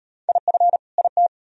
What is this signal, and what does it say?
Morse code, heard as a keyed tone, about 38 wpm, 700 Hz. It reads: IF IT